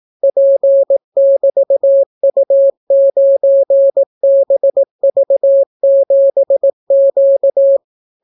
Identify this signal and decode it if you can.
Morse code, heard as a keyed tone, about 18 wpm, 550 Hz. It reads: P<BT>U9BV7Q